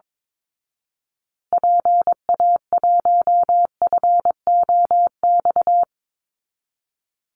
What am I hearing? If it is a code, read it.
Morse code, heard as a keyed tone, about 22 wpm, 700 Hz. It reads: PA1FOX
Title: Call Signs Repeated 2x - 22wpm